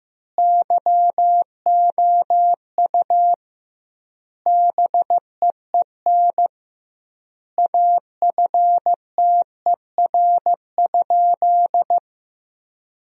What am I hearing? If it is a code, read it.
Morse code, heard as a keyed tone, about 15 wpm, 700 Hz. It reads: YOU BEEN AFTER?
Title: 2x Word-Spacing - Sentences from Top 200 Words - 15wpm